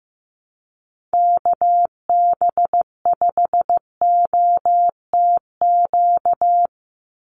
Morse code, heard as a keyed tone, about 15 wpm, 700 Hz. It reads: KB5OTQ
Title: USA Call Signs 15wpm